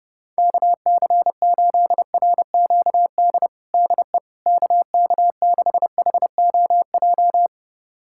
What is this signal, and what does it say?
Morse code, heard as a keyed tone, about 30 wpm, 700 Hz. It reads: KC8RQD DE KK6HOJ
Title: USA Callsign DE Callsign - 6 Characters - 30wpm